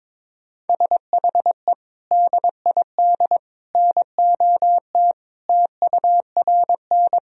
Morse code, heard as a keyed tone, about 21 wpm, 700 Hz. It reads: SHE DID NOT TURN
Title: Sentences from Top 1000 Words - Review Part 1 - 22wpm